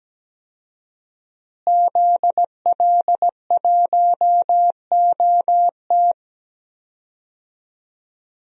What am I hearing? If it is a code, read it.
Morse code, heard as a keyed tone, about 17 wpm, 700 Hz. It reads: ZL1OT
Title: DXCC-100 Call Signs - 5 Characters - Repeated 2x - 17wpm